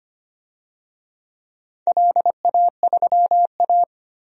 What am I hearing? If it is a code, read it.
Morse code, heard as a keyed tone, about 25 wpm, 690 Hz. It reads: LA3A
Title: DXCC-100 Call Signs - 4 Characters - 25wpm